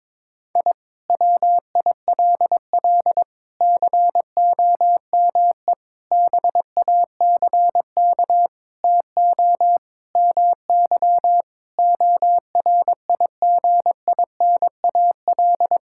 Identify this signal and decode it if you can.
Morse code, heard as a keyed tone, about 22 wpm, 700 Hz. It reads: I WILL COME BACK TO MY ORIGINAL
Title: Sentences from Top 1000 Words - Review Part 6 - 22wpm